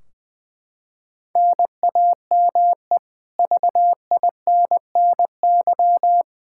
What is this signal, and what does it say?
Morse code, heard as a keyed tone, about 21 wpm, 700 Hz. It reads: NAME VINNY
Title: Practice QSO - Exchange 1 - Beginning - 20wpm